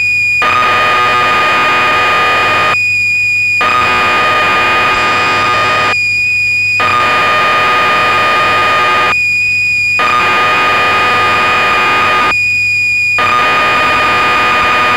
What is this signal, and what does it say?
Cassette data (Kansas City type).
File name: magazines / Acorn User / 1984-12 BBC Programs